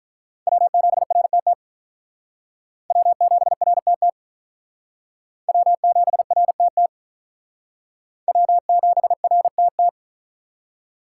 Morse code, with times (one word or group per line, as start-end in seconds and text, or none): 0.5-1.5 W7RTT
2.9-4.1 W7RTT
5.5-6.9 W7RTT
8.3-9.9 W7RTT